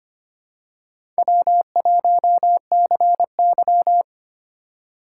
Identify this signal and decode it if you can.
Morse code, heard as a keyed tone, about 25 wpm, 700 Hz. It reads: W1CY
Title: USA Call Signs - 4 Characters - 4x Character Spacing - 25wpm